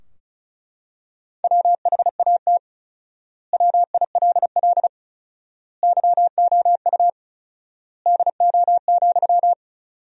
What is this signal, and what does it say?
Morse code, heard as a keyed tone, about 36 wpm, 700 Hz. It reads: WHAT WILL YOU DO,
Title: 4x Word-Spacing - Sentences from Top 100 Words - 35wpm